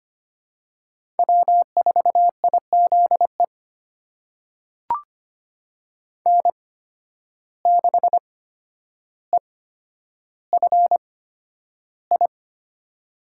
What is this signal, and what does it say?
Morse code, heard as a keyed tone, about 25 wpm, spaced as for 11 wpm, 700 Hz. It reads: W4IZE N6EFI
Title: USA Call Signs - 5 Characters - 8x Character Spacing - 25wpm